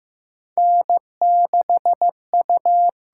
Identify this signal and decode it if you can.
Morse code, heard as a keyed tone, about 15 wpm, 700 Hz. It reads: N6U
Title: USA Call Signs - 6x Character Spacing - 15wpm